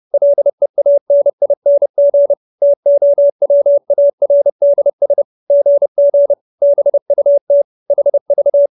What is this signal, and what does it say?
Morse code, heard as a keyed tone, about 30 wpm, 570 Hz. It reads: LEANING TOWARDS GG BUT HV